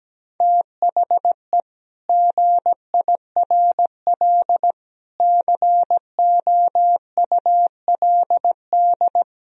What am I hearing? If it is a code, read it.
Morse code, heard as a keyed tone, about 17 wpm, 700 Hz. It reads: THE GIRL COULD
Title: Sentences from Top 1000 Words - Review Part 4 - 17wpm